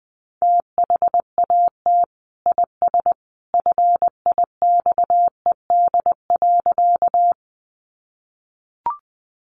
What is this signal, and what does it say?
Morse code, heard as a keyed tone, about 20 wpm, 700 Hz. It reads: THAT IS FIXED.
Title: Sentences from Top 1100 Words 20wpm